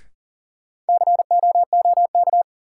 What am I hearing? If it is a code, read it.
Morse code, heard as a keyed tone, about 40 wpm, 700 Hz. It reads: COOK
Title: Sentences from Top 700 Words - New Word Drill - 40wpm